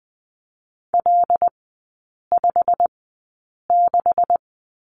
Morse code, heard as a keyed tone, about 20 wpm, spaced as for 8 wpm, 690 Hz. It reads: L56
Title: 145 Lesson 24e - Letter y - Three Characters 20wpm